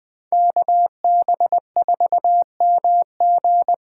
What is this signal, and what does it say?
Morse code, heard as a keyed tone, about 20 wpm, 700 Hz. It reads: KB4MG
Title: USA Call Signs - 8x Character Spacing - 20wpm